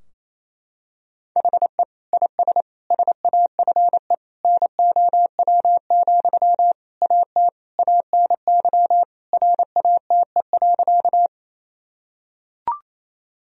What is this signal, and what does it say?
Morse code, heard as a keyed tone, about 28 wpm, 700 Hz. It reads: HE IS SAFE NOW, AT ANY RATE.
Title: Sentences from Top 1000 Words - Review Part 4 - 28wpm